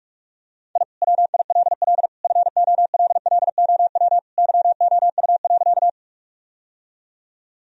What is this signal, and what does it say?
Morse code, heard as a keyed tone, about 41 wpm, 700 Hz. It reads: I WILL FOLLOW YOU.
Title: Sentences from Top 1000 Words - Review Part 1 - 45wpm